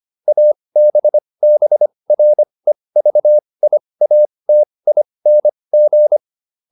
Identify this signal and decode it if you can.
Morse code, heard as a keyed tone, about 25 wpm, spaced as for 20 wpm, 600 Hz. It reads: ABBREVIATING